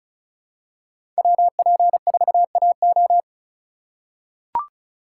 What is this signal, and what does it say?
Morse code, heard as a keyed tone, about 36 wpm, 700 Hz. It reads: WP4AO
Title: USA Call Signs - 5 Characters - 3x Character Spacing - 35wpm